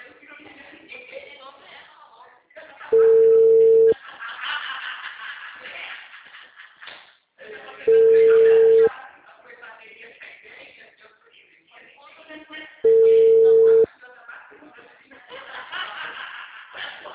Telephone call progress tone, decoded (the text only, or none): ringback tone